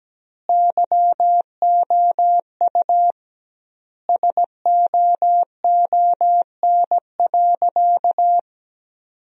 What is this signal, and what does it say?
Morse code, heard as a keyed tone, about 17 wpm, 700 Hz. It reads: YOU SOON.
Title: 2x Word-Spacing - Sentences from Top 400 Words - 17wpm